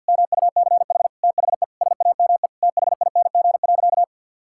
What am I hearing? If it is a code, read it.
Morse code, heard as a keyed tone, about 51 wpm, 700 Hz. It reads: MUCH THE SAME THING.
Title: Sentences from Top 1600 Words 50wpm